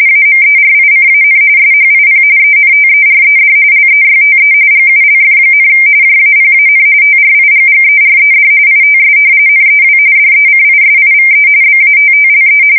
RTTY, about 45.45 baud. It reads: AMERICAN SHIP ENTERED LEYTE GULF, JAPAN TRIGGERED ITS NEW DECISIVE BATTLE
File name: fsk170 / battleshipnj